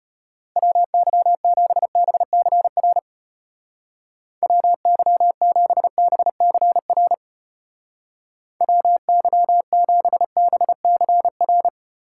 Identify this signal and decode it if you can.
Morse code, heard as a keyed tone, about 31 wpm, 700 Hz. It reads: WY7BCR WY7BCR WY7BCR
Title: Speed Racing - USA 2x3 Call Signs 25wpm